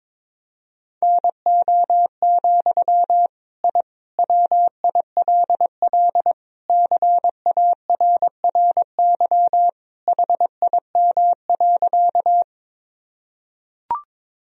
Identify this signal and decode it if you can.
Morse code, heard as a keyed tone, about 22 wpm, 700 Hz. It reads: NO, I WILL CARRY HIM.
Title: Sentences from Top 1000 Words - Review Part 1 - 22wpm